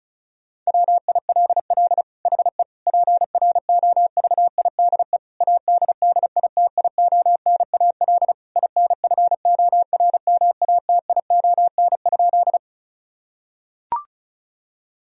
Morse code, heard as a keyed tone, about 36 wpm, 700 Hz. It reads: WILL HE PROVIDE ADDITIONAL INFORMATION?
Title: Sentences from Top 1000 Words - Review Part 4 - 35wpm